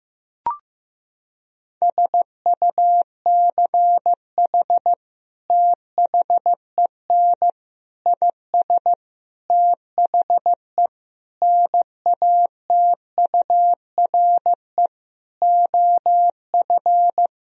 Morse code, heard as a keyed tone, about 15 wpm, 700 Hz. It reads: SUCH THEN IS THE NATURE OF
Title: Sentences from Top 1000 Words - Review Part 4 - 15wpm